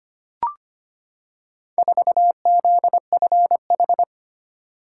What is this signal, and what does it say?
Morse code, heard as a keyed tone, about 25 wpm, 700 Hz. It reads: 4ZFH
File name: practice / 25wpm / Four Letter-Number Combinations - 25wpm